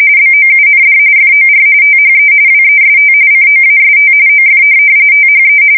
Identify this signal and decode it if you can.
RTTY, about 45.45 baud. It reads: BOUND FOR THE WEST COAST OF THE
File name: fsk170 / USS-Missouri